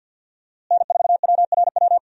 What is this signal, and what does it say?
Morse code, heard as a keyed tone, about 52 wpm, 700 Hz. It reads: N4WRW
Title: USA Call Signs - 5 Characters - Repeated 3x - 50wpm